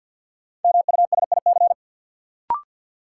Morse code, heard as a keyed tone, about 53 wpm, 700 Hz. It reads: MUSIC
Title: Sentences from Top 300 Words - New Word Drill - 50wpm